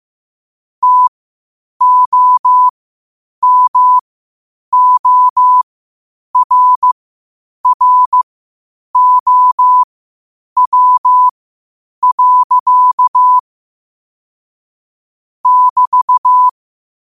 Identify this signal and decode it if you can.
Morse code, heard as a keyed tone, about 15 wpm, spaced as for 8 wpm, 1000 Hz. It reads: TOMORROW. <BT>
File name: qotdfiles / 8wpm